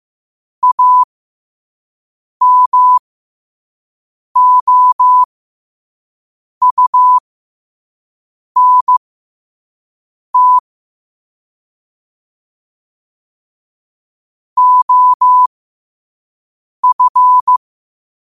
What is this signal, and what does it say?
Morse code, heard as a keyed tone, about 15 wpm, spaced as for 5 wpm, 1000 Hz. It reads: AMOUNT OF
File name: qotdfiles / 5wpm